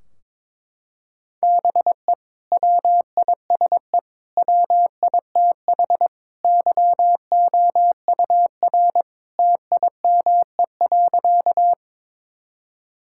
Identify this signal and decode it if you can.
Morse code, heard as a keyed tone, about 21 wpm, 700 Hz. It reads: BE WISE WITH YOUR TIME.
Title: Sentences from Top 1600 Words 22wpm